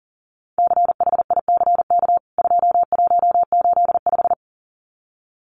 Morse code, heard as a keyed tone, about 40 wpm, 700 Hz. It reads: CHICK 2185
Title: CWT Exchange 40wpm